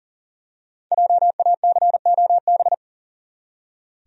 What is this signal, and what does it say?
Morse code, heard as a keyed tone, about 39 wpm, 700 Hz. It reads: JACOB
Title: USA Common Names - 40wpm